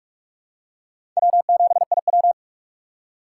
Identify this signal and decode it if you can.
Morse code, heard as a keyed tone, about 42 wpm, 700 Hz. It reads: W7IW
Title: USA Call Signs - 4 Characters - 6x Character Spacing - 45wpm